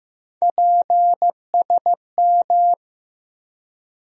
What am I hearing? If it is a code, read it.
Morse code, heard as a keyed tone, about 15 wpm, 700 Hz. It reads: PSM
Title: USA Call Sign Postfixes - 3 Letters 15wpm